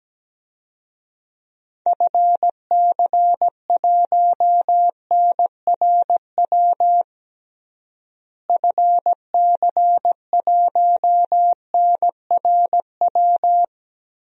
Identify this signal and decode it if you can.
Morse code, heard as a keyed tone, about 17 wpm, 700 Hz. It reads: FC1NRW FC1NRW
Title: DXCC-100 Call Signs - 6 Characters - Repeated 3x - 17wpm